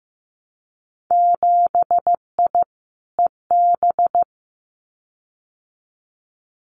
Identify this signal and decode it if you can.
Morse code, heard as a keyed tone, about 15 wpm, 700 Hz. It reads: 7I EB
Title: Winter Field Day Exchanges Encore 15wpm